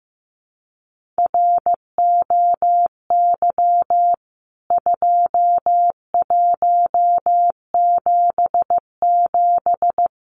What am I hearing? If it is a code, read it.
Morse code, heard as a keyed tone, about 15 wpm, 700 Hz. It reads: ROY 2177
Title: CWT Exchange 15wpm